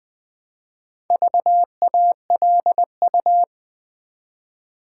Morse code, heard as a keyed tone, about 20 wpm, 700 Hz. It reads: VALU
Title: Build A Word - 1 Letter At A Time - 5-Letter Words - 20wpm